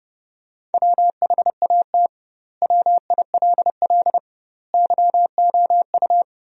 Morse code, heard as a keyed tone, about 30 wpm, 700 Hz. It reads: WHAT WILL YOU
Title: 2x Word-Spacing - Sentences from Top 200 Words - 30wpm